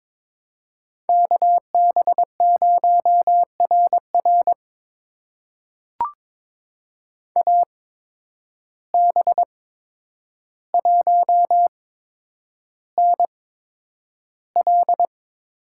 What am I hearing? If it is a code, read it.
Morse code, heard as a keyed tone, about 22 wpm, spaced as for 9 wpm, 700 Hz. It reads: KB0RR AB1NL
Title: USA Call Signs - 5 Characters - 8x Character Spacing - 22wpm